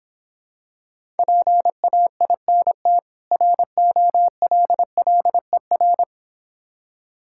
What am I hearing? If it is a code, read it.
Morse code, heard as a keyed tone, about 26 wpm, 700 Hz. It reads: PAINT ROLLER